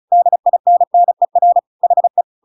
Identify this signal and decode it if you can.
Morse code, heard as a keyed tone, about 35 wpm, 700 Hz. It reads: DINNER HE
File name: Pinocchio / Pinocchio ch8 35 35 ik1hge